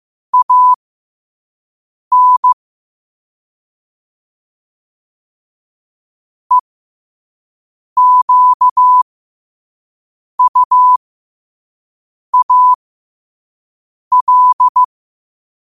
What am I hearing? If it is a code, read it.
Morse code, heard as a keyed tone, about 15 wpm, spaced as for 5 wpm, 1000 Hz. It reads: AN EQUAL
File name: qotdfiles / 5wpm